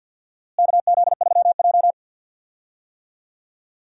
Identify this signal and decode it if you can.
Morse code, heard as a keyed tone, about 52 wpm, 700 Hz. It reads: KZ3J
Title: USA Call Signs - 4 Characters - Repeated 3x - 50wpm